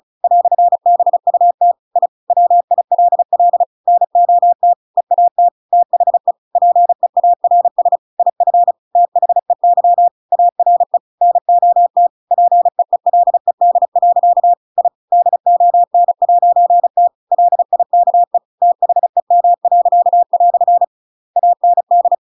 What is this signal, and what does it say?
Morse code, heard as a keyed tone, about 34 wpm, 700 Hz. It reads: "BUT I WILL NOT EAT THE PEARS IF THEY ARE NOT PEELED. I DON'T LIKE THEM." AND
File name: Pinocchio / Pinocchio ch7 35 35 ik1hge